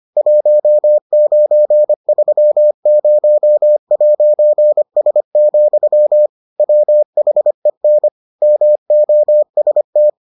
Morse code, heard as a keyed tone, about 25 wpm, 600 Hz. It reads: 1930'S, WHEN MOST